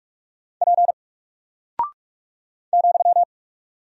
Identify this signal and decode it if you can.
Morse code, heard as a keyed tone, about 44 wpm, 700 Hz. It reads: P ,
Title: Single Character (Letter:Number:Punctuation:Prosign) 45wpm